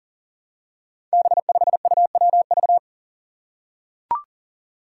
Morse code, heard as a keyed tone, about 40 wpm, 700 Hz. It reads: B5UWV
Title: Five Challenging Letter-Number Combinations 40wpm